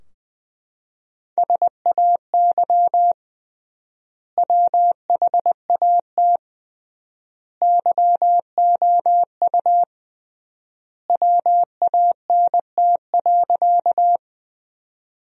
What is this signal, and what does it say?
Morse code, heard as a keyed tone, about 21 wpm, 700 Hz. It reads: SAY WHAT YOU WANT.
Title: 3x Word-Spacing - Sentences from Top 200 Words - 20wpm